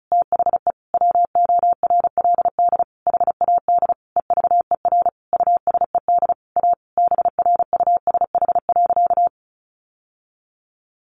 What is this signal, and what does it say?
Morse code, heard as a keyed tone, about 34 wpm, 700 Hz. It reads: THE WORLD HAD EVER USED A BRUSH.
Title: Sentences from Top 1100 Words 35wpm